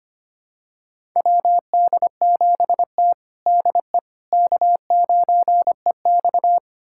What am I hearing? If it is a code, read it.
Morse code, heard as a keyed tone, about 25 wpm, 700 Hz. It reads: WD7T DE K9EX
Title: USA Callsign DE Callsign - 4 Characters - 25wpm